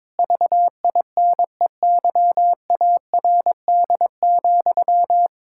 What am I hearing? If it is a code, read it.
Morse code, heard as a keyed tone, about 22 wpm, 700 Hz. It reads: VINEYARD,
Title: Pangram Challenge v2 - 22wpm